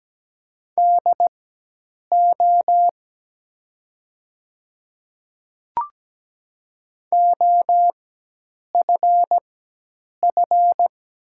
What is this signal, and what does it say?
Morse code, heard as a keyed tone, about 17 wpm, spaced as for 8 wpm, 700 Hz. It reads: DO OFF
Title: 4x Character Spacing - Top 200 Words - 17wpm